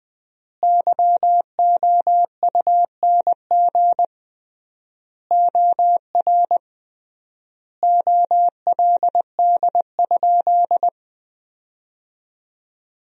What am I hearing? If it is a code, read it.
Morse code, heard as a keyed tone, about 20 wpm, 700 Hz. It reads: YOUNG OR OLD?
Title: 3x Word-Spacing - Sentences from Top 400 Words - 20wpm